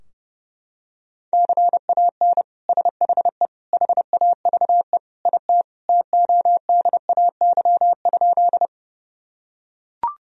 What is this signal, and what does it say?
Morse code, heard as a keyed tone, about 30 wpm, 700 Hz. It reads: CAN SHE HAVE IT TODAY?
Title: Sentences from Top 1000 Words - Review Part 4 - 30wpm